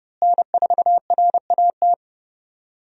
Morse code, heard as a keyed tone, about 30 wpm, 700 Hz. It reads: N4RAT
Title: USA Call Signs - 5 Characters - Repeated 2x - 30wpm